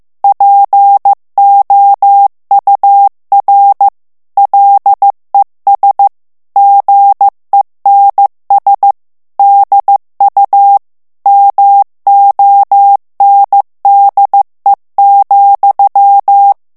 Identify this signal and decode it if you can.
Morse code, heard as a keyed tone, about 15 wpm, 800 Hz. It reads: POUR LES GENS DU MONDE,
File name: morse mp3 / FRANCAIS TEXTE CLAIR 2 VITESSE 14 MOTS MINUTES